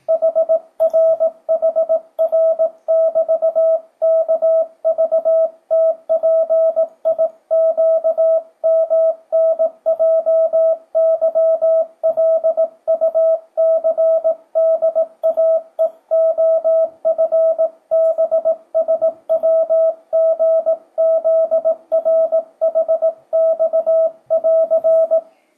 Morse code, heard as a keyed tone, about 18 wpm, 650 Hz, with radio noise. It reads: HRHR<BT>KVTPIQMNJYLUCDAEOFBSWGZRHX<AR>
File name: 2025cw / speedup / 18wpm